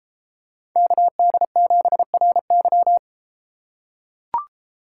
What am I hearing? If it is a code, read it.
Morse code, heard as a keyed tone, about 32 wpm, 700 Hz. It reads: KD7RY